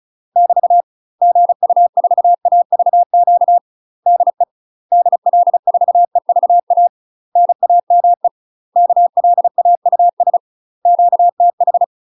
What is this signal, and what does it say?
Morse code, heard as a keyed tone, about 35 wpm, 700 Hz. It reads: <BT> GU4AVQ DE DL4EVA NAME KLAUS QTH
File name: UK MASTER QSOs TRAINING / 35WPM / UK MASTER QSOs 0017